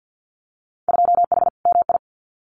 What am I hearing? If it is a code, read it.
Morse code, heard as a keyed tone, about 50 wpm, 700 Hz. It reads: 2H MI
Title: Winter Field Day Exchanges Encore 50wpm